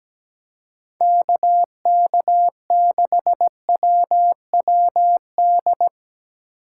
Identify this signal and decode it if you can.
Morse code, heard as a keyed tone, about 19 wpm, 700 Hz. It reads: KK6WWD
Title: Speed Racing - USA 2x3 Call Signs 17wpm